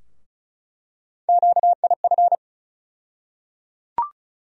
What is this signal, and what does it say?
Morse code, heard as a keyed tone, about 35 wpm, 700 Hz. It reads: QIF
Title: USA Call Sign Postfixes - 3 Letters 35wpm